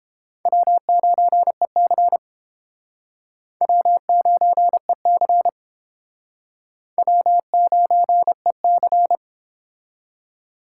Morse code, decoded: W9EC W9EC W9EC

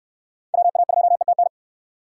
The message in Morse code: WA2ETN

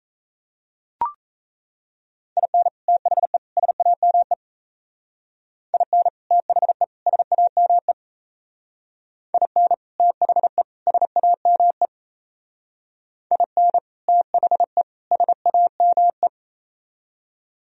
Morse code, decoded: IN THE SAME IN THE SAME IN THE SAME IN THE SAME

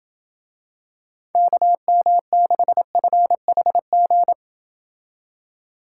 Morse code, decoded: KM6FHG

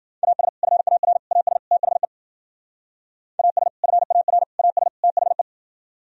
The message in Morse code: AS FAR AS THE AS FAR AS THE

